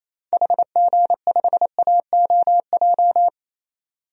Morse code, heard as a keyed tone, about 28 wpm, 700 Hz. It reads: HG5AOJ
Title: DXCC-100 Call Signs - 6 Characters - 28wpm